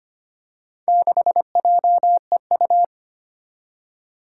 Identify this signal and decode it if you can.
Morse code, heard as a keyed tone, about 25 wpm, 700 Hz. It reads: 6JEU
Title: Four Challenging Letter-Number Combinations 25wpm